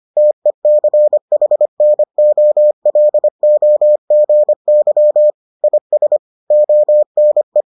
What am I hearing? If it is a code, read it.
Morse code, heard as a keyed tone, about 25 wpm, 600 Hz. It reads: TECHNOLOGY IS ONE